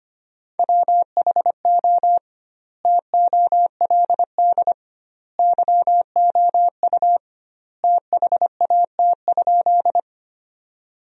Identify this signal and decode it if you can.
Morse code, heard as a keyed tone, about 25 wpm, 700 Hz. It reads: WHO TOLD YOU THAT?
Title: 2x Word-Spacing - Sentences from Top 400 Words - 25wpm